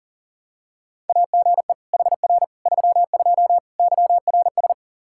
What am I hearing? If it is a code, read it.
Morse code, heard as a keyed tone, about 40 wpm, 700 Hz. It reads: AGE HR 32 YRS
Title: Practice QSO - Exchange 3 - More Info - 40wpm